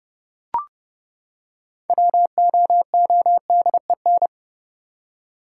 Morse code, heard as a keyed tone, about 32 wpm, 700 Hz. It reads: WOODEN
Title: Six Letter Words - 30wpm